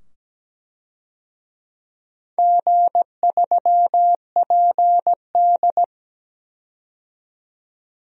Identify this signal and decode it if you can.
Morse code, heard as a keyed tone, about 17 wpm, 700 Hz. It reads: G3PD
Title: DXCC-100 Call Signs - 4 Characters - 17wpm